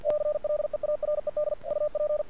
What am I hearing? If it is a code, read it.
Morse code, heard as a keyed tone, about 50 wpm, 600 Hz, with radio noise. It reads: CLEARED UP
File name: archived / CRSnet